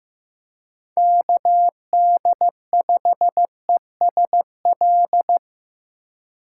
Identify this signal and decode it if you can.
Morse code, heard as a keyed tone, about 15 wpm, 700 Hz. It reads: KD5ESL
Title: USA Call Signs - 6 Characters - 2x Character Spacing - 15wpm